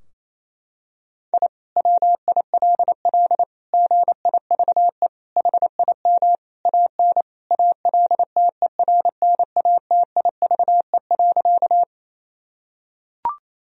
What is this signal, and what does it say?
Morse code, heard as a keyed tone, about 28 wpm, 700 Hz. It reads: I WILL GIVE HIM AN ALTERNATIVE.
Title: Sentences from Top 1000 Words - Review Part 4 - 28wpm